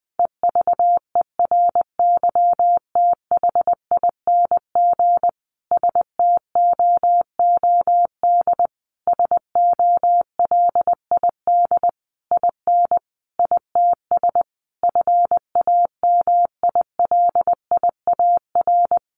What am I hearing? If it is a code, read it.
Morse code, heard as a keyed tone, about 20 wpm, 700 Hz. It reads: EVERYTHING STOOD SOLID IN ITS FAMILIAR
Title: Sentences from Top 1500 Words 20wpm